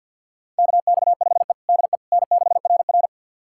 Morse code, heard as a keyed tone, about 52 wpm, 700 Hz. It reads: KX5E DE N6RR
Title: USA Callsign DE Callsign - 4 Characters - 50wpm